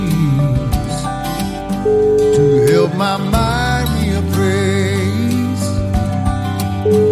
Ringback tone (Europe, 425 Hz), with noise.